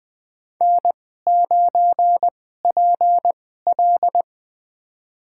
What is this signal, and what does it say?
Morse code, heard as a keyed tone, about 20 wpm, spaced as for 15 wpm, 700 Hz. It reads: N9PL
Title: USA Call Signs - 4 Characters - 2x Character Spacing - 20wpm